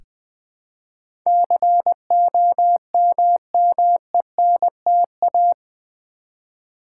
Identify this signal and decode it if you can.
Morse code, heard as a keyed tone, about 20 wpm, 700 Hz. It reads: COMMENTA
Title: Build A Word - 4 Letters At A Time - 11-Letter Words - 20wpm